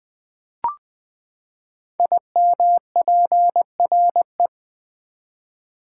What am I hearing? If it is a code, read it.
Morse code, heard as a keyed tone, about 20 wpm, 700 Hz. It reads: IMPRE
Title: Build A Word - 5 Letters At A Time - 10-Letter Words - 20wpm